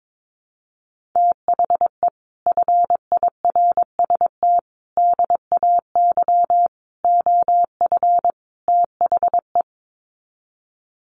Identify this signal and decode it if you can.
Morse code, heard as a keyed tone, about 22 wpm, 700 Hz. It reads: THE FIRST DAY OF THE
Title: Sets of 5 Words Encore2 22wpm